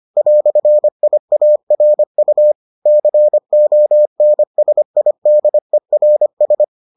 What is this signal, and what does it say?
Morse code, heard as a keyed tone, about 25 wpm, 600 Hz. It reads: "IARU CONSIDERS